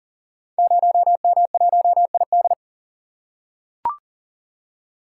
Morse code, heard as a keyed tone, about 38 wpm, 700 Hz. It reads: 0M1ID
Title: Code Groups - 5 Letter:Numbers - 40wpm